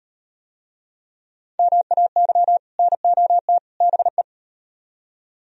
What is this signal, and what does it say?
Morse code, heard as a keyed tone, about 36 wpm, 700 Hz. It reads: MAY NOT BE